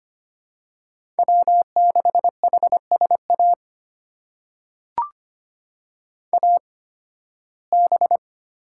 Morse code, heard as a keyed tone, about 25 wpm, spaced as for 16 wpm, 700 Hz. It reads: W6HSA AB
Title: USA Call Signs - 8x Character Spacing - 25wpm